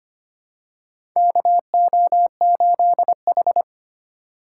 Morse code, heard as a keyed tone, about 25 wpm, 700 Hz. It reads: KO8H
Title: USA Call Signs - 4x Character Spacing - 25wpm